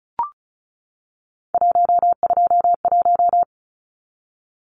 Morse code, heard as a keyed tone, about 34 wpm, 700 Hz. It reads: 121
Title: Numbers - 3-Digit 35wpm